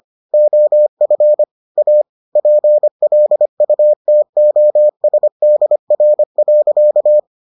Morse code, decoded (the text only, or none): OF A PLUTOSDR.